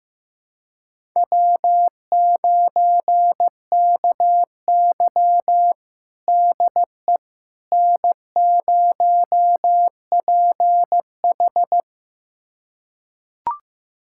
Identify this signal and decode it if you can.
Morse code, heard as a keyed tone, about 16 wpm, 700 Hz. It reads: W9KY DE N0PH
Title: USA Callsign DE Callsign - 4 Characters - 15wpm